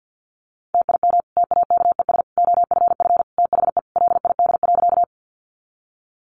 Morse code, heard as a keyed tone, about 50 wpm, 700 Hz. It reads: TIM TAKES OFF THE LID.
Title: Easy Phrases 50wpm